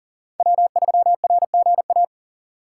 Morse code, heard as a keyed tone, about 40 wpm, 700 Hz. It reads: W3RGA
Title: K1USN SST QSO Practice - 40wpm